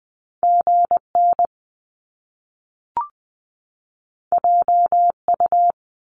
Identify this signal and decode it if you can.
Morse code, heard as a keyed tone, about 20 wpm, 700 Hz. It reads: GN JU
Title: Top Two Letter Combinations in Words 20wpm